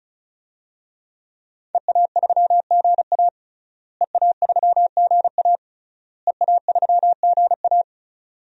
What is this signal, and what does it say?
Morse code, heard as a keyed tone, about 36 wpm, 700 Hz. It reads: EA3GA EA3GA EA3GA
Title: DXCC-100 Call Signs - 5 Characters - Repeated 3x - 35wpm